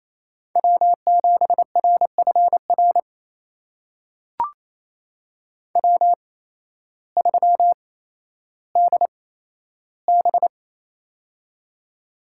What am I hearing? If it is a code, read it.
Morse code, heard as a keyed tone, about 28 wpm, spaced as for 13 wpm, 700 Hz. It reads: W7RFR W3DB
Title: USA Call Signs - 8x Character Spacing - 28wpm